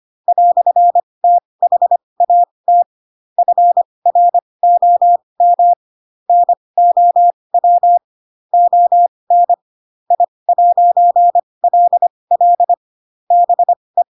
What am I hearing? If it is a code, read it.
Morse code, heard as a keyed tone, about 25 wpm, spaced as for 20 wpm, 700 Hz. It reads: "THAT FROM NOW ON I'LL BE
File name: Pinocchio / Pinocchio ch8 15 25 ik1hge